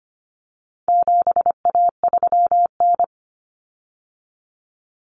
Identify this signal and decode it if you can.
Morse code, heard as a keyed tone, about 25 wpm, 700 Hz. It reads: 7A3N
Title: Common Cut Numbers - 4-Digit 25wpm